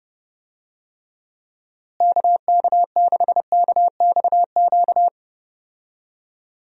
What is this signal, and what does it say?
Morse code, heard as a keyed tone, about 30 wpm, 700 Hz. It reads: KK6KXQ
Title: USA Call Signs - 6 Characters - Repeated 2x - 30wpm